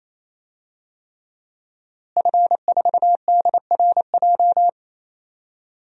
Morse code, heard as a keyed tone, about 28 wpm, 700 Hz. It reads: F4DRJ
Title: DXCC-100 Call Signs - 5 Characters - Repeated 3x - 28wpm